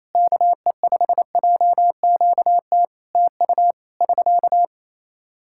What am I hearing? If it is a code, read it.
Morse code, heard as a keyed tone, about 27 wpm, 700 Hz. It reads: KE5JQT TU <SK>
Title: Practice QSO - Exchange 4 - Ending - 28wpm